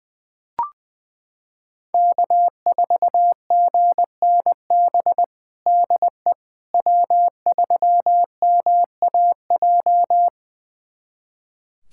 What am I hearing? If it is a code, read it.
Morse code, heard as a keyed tone, about 20 wpm, 700 Hz. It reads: K4GNB DE W3MAJ
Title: USA Callsign DE Callsign - 5 Characters - 20wpm